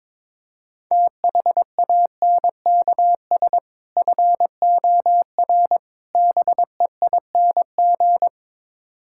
Morse code, heard as a keyed tone, about 22 wpm, 700 Hz. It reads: THANKS FOR BEING
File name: practice / 22wpm / Build A Phrase - 5-Word Phrases - 22wpm